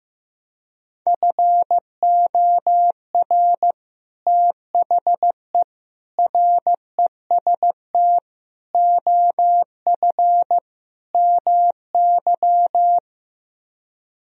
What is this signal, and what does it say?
Morse code, heard as a keyed tone, about 15 wpm, 700 Hz. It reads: FOR THE REST OF MY
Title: Speed-Racing - Sets of 5 Words 15wpm